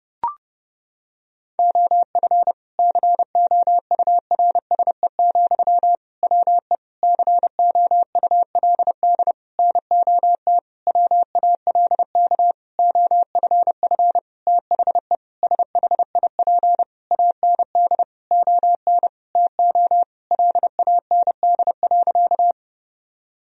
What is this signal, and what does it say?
Morse code, heard as a keyed tone, about 30 wpm, 700 Hz. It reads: OF COURSE, WE COULD NOT WALK OFF THE SHIP AND ON TO LAND.
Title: Sentences from Top 1000 Words - Review Part 2 - 30wpm